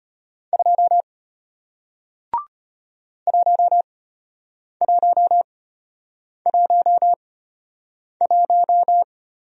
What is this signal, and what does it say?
Morse code, heard as a keyed tone, about 33 wpm, spaced as for 8 wpm, 700 Hz. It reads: 2 1111